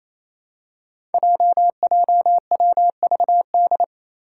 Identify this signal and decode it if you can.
Morse code, heard as a keyed tone, about 27 wpm, 700 Hz. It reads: JJWVD
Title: Code Groups - 5 Letters - 28wpm